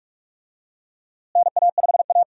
Morse code, heard as a keyed tone, about 41 wpm, 700 Hz. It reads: NA5A
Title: USA Call Signs - 4 Characters - Repeated 2x - 45wpm